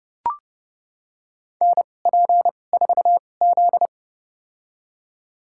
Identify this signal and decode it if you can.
Morse code, heard as a keyed tone, about 30 wpm, spaced as for 22 wpm, 700 Hz. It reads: NP4Z
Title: USA Call Signs - 2x Character Spacing - 30wpm